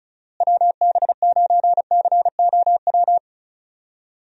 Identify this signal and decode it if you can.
Morse code, heard as a keyed tone, about 34 wpm, 700 Hz. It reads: WB9COW
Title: USA Call Signs - 6 Characters - Repeated 2x - 35wpm